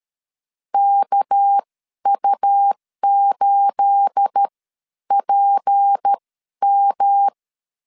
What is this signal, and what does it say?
Morse code, heard as a keyed tone, about 13 wpm, spaced as for 10 wpm, 800 Hz. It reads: KU8PM
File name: LEZIONI / 9 28 12wpm 2351